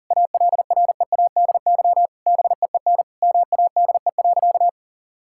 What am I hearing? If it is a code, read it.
Morse code, heard as a keyed tone, about 37 wpm, 700 Hz. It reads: ALREADY BEEN MADE.
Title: Sentences from Top 1000 Words - Review Part 5 - 40wpm